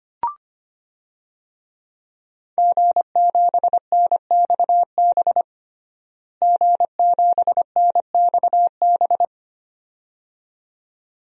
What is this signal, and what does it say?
Morse code, heard as a keyed tone, about 25 wpm, 700 Hz. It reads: G7NXB G7NXB
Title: DXCC-100 Call Signs - 5 Characters - Repeated 2x - 25wpm